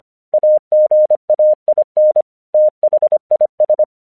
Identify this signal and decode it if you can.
Morse code, heard as a keyed tone, about 25 wpm, 610 Hz. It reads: AGAIN THIS
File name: 1 English 25wpm / AQC15